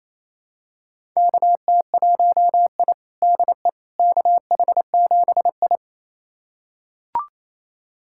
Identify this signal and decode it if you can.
Morse code, heard as a keyed tone, about 28 wpm, 700 Hz. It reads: KT1I DE KH7I
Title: USA Callsign DE Callsign - 4 Characters - 28wpm